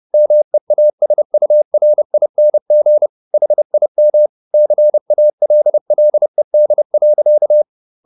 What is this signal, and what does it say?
Morse code, heard as a keyed tone, about 30 wpm, 600 Hz. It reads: MEASURING HIM CALLED.